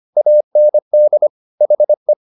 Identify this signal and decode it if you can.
Morse code, heard as a keyed tone, about 25 wpm, 600 Hz. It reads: AND HE